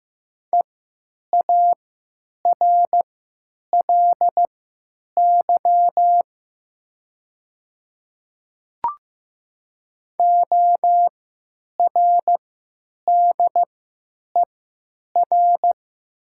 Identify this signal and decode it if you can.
Morse code, heard as a keyed tone, about 15 wpm, spaced as for 9 wpm, 700 Hz. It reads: EARLY ORDER
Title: 3x Character Spacing - Top 500 Words - 15wpm